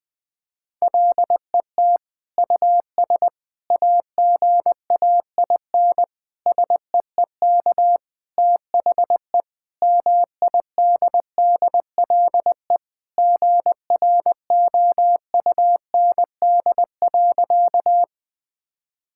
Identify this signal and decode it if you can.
Morse code, heard as a keyed tone, about 21 wpm, 700 Hz. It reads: LET US AGAIN SEEK THE MIDDLE GROUND.
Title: Sentences from Top 1600 Words 20wpm